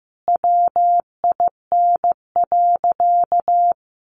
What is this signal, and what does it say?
Morse code, heard as a keyed tone, about 15 wpm, 700 Hz. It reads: WIN.
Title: Sentences from Top 1500 Words 15wpm